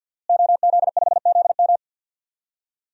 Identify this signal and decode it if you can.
Morse code, heard as a keyed tone, about 50 wpm, 700 Hz. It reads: OZ5ZM
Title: DXCC-100 Call Signs - 5 Characters - Repeated 3x - 50wpm